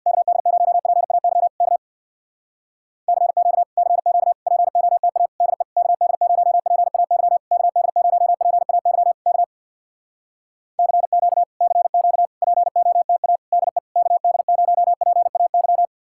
Morse code, heard as a keyed tone, about 64 wpm, 700 Hz. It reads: KD0PAY K CQ CQ POTA DE KD0PAY KD0PAY K CQ CQ POTA DE KD0PAY